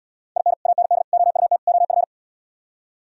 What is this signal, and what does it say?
Morse code, heard as a keyed tone, about 76 wpm, 700 Hz. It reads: IN AND OUT OF